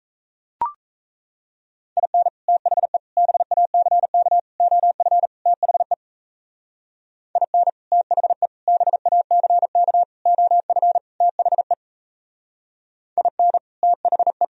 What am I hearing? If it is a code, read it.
Morse code, heard as a keyed tone, about 35 wpm, 700 Hz. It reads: IN THE BACK OF THE IN THE BACK OF THE IN THE